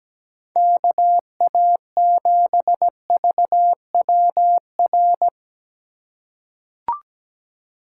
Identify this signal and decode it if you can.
Morse code, heard as a keyed tone, about 17 wpm, 700 Hz. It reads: KA7VWR
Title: USA Call Signs - 6 Characters - 3x Character Spacing - 17wpm